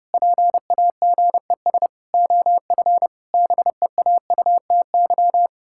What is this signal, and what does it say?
Morse code, heard as a keyed tone, about 30 wpm, 700 Hz. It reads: PAGES OF BEAUTY
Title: Pangram Challenge v2 - 30wpm